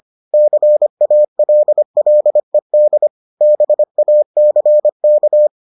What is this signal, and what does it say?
Morse code, heard as a keyed tone, about 25 wpm, 600 Hz. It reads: CALLED BACK